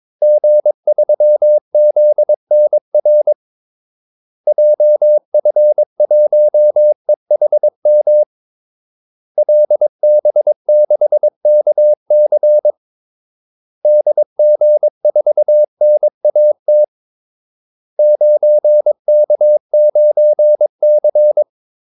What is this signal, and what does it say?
Morse code, heard as a keyed tone, about 22 wpm, 600 Hz. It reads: G3ZNR JF1EHM LB6KC DG4NAT 9K9C
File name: Callsigns-22wpm